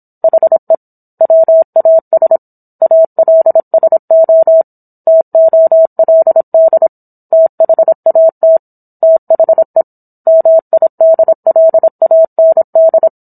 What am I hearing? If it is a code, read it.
Morse code, heard as a keyed tone, about 27 wpm, 650 Hz. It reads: HE WAS ALSO TOLD THAT THE MIDLAND